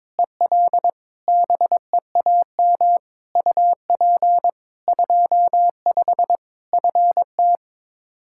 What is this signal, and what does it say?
Morse code, heard as a keyed tone, about 22 wpm, 700 Hz. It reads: EL BEAM UP 25 FT
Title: Practice QSO - Exchange 2 - Info - 22wpm